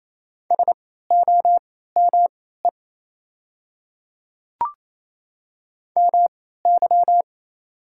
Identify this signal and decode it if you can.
Morse code, heard as a keyed tone, about 28 wpm, spaced as for 16 wpm, 700 Hz. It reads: SOME MY
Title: 3x Character Spacing - Top 200 Words - 28wpm